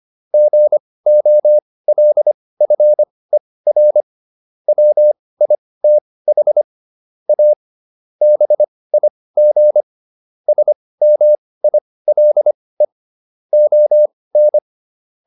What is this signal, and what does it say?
Morse code, heard as a keyed tone, about 25 wpm, spaced as for 18 wpm, 600 Hz. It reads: GOLFER WITH A BIG SMILE ON